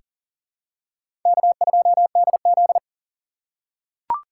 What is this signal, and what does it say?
Morse code, heard as a keyed tone, about 38 wpm, 700 Hz. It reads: K2DZ
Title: USA Call Signs - 4 Characters - 6x Character Spacing - 40wpm